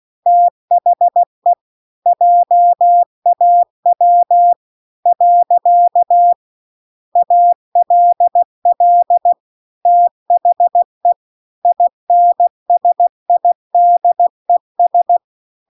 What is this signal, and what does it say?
Morse code, heard as a keyed tone, about 16 wpm, 700 Hz. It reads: THE JAW . ALL THE INSIDES